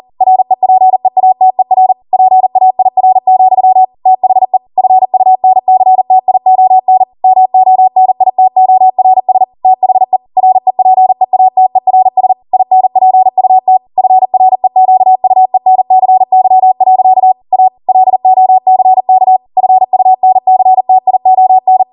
Morse code, heard as a keyed tone, about 40 wpm, 750 Hz. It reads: REPEATER PAIR, THE FUNCTION MONITORS THE REPEATERS INPUT FREQUENCY. A LOCK FUNCTION
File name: Morse / 40WPM